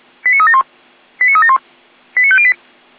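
Five-tone selective call (ZVEI), heard as quick runs of five tones, tones of about 70 ms. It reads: 89371 89371 89508
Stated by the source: ZVEI five-tone call 89136 received by SDR.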